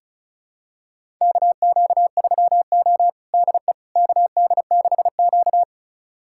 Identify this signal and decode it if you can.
Morse code, heard as a keyed tone, about 36 wpm, 700 Hz. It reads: KQ3O DE KD6Q
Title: USA Callsign DE Callsign - 4 Characters - 35wpm